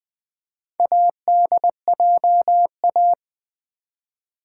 Morse code, heard as a keyed tone, about 20 wpm, 700 Hz. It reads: ADJA
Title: Build A Word - 4 Letters At A Time - 8-Letter Words - 20wpm